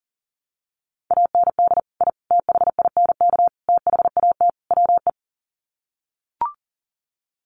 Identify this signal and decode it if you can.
Morse code, heard as a keyed tone, about 40 wpm, 700 Hz. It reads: AND I THINK THAT WE
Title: Sets of 5 Words Encore2 40wpm